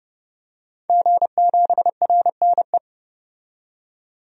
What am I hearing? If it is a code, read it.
Morse code, heard as a keyed tone, about 31 wpm, 700 Hz. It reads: G7RNE
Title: DXCC-100 Call Signs - 5 Characters - 30wpm